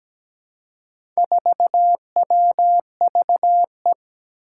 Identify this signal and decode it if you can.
Morse code, heard as a keyed tone, about 17 wpm, 700 Hz. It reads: 4WVE
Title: Four Challenging Letter-Number Combinations 17wpm